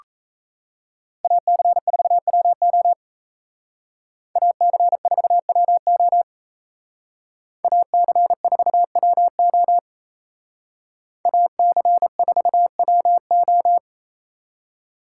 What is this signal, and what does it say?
Morse code, heard as a keyed tone, about 36 wpm, 700 Hz. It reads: AC4WO AC4WO AC4WO AC4WO